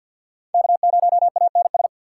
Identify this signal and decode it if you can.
Morse code, heard as a keyed tone, about 50 wpm, 700 Hz. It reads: K0ANS
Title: USA Call Signs - 5 Characters - 4x Character Spacing - 50wpm